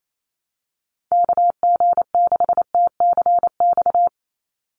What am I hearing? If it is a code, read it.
Morse code, heard as a keyed tone, about 28 wpm, 700 Hz. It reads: KG6TCX
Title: USA Call Signs 28wpm